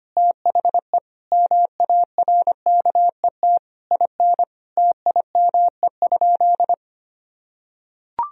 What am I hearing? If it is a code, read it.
Morse code, heard as a keyed tone, about 25 wpm, 700 Hz. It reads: THE MARKET IN TIME?
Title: Sentences from Top 1000 Words - Review Part 3 - 25wpm